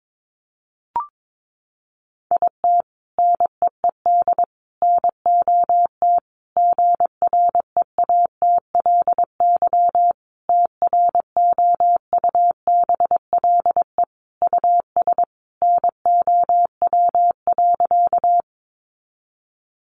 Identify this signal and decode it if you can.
Morse code, heard as a keyed tone, about 22 wpm, 700 Hz. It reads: IT NEED NOT GREATLY TROUBLE US NOW.
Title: Sentences from Top 1400 Words 22wpm